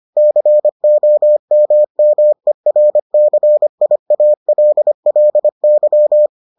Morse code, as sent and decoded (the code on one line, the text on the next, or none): -.-. --- -- -- . .-. -.-. .. .- .-.. .-.. -.--
COMMERCIALLY